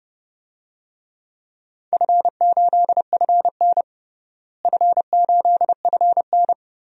Morse code, heard as a keyed tone, about 29 wpm, 700 Hz. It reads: F8FN F8FN
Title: DXCC-100 Call Signs - 4 Characters - Repeated 2x - 30wpm